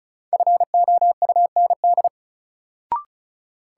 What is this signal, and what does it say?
Morse code, heard as a keyed tone, about 35 wpm, 700 Hz. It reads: FOUND
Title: Sentences from Top 300 Words - New Word Drill - 35wpm